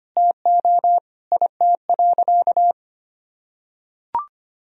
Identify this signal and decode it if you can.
Morse code, heard as a keyed tone, about 25 wpm, 700 Hz. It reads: TO IT.
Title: Sentences from Top 1000 Words - Review Part 2 - 25wpm